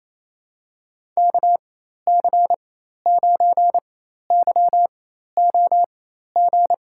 Morse code, heard as a keyed tone, about 28 wpm, spaced as for 13 wpm, 700 Hz. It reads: KC9YOG
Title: USA Call Signs - 6 Characters - 4x Character Spacing - 28wpm